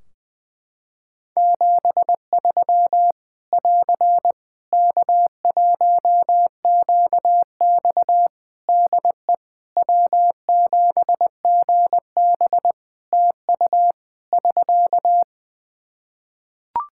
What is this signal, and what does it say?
Morse code, heard as a keyed tone, about 21 wpm, 700 Hz. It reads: 73 <AR> K1QX DE W7GB TU <SK>
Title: K1USN SST QSO Practice 20wpm